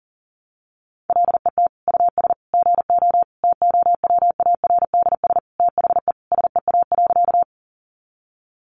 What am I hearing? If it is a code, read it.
Morse code, heard as a keyed tone, about 37 wpm, 700 Hz. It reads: LET US GO TOWARDS THE SEA.
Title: Sentences from Top 1100 Words 40wpm